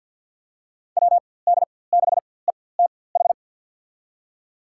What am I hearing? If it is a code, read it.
Morse code, heard as a keyed tone, about 52 wpm, spaced as for 23 wpm, 700 Hz. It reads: WD6ETH
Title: USA Call Signs - 6 Characters - 4x Character Spacing - 50wpm